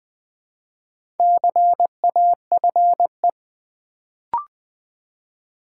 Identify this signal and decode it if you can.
Morse code, heard as a keyed tone, about 20 wpm, 700 Hz. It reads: CAFE
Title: Build A Word - 2 Letters At A Time - 4-Letter Words - 20wpm